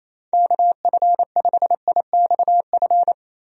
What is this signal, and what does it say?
Morse code, heard as a keyed tone, about 28 wpm, 700 Hz. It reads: KF5IXF
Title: USA Call Signs - 6 Characters - 28wpm